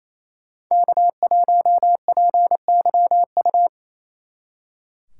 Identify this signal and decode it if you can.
Morse code, heard as a keyed tone, about 35 wpm, 700 Hz. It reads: K1PYU
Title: Speed Racing - USA 1x3 Call Signs 28wpm